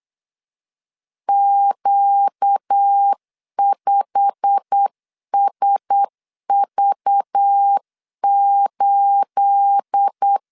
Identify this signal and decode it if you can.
Morse code, heard as a keyed tone, about 8 wpm, 800 Hz. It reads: Q5SV8
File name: LEZIONI / 9 28 8wpm 3512